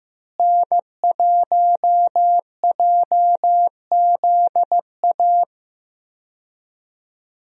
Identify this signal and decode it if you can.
Morse code, heard as a keyed tone, about 15 wpm, 700 Hz. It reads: N1JZA
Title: USA Call Signs - 5 Characters - Repeated 2x - 15wpm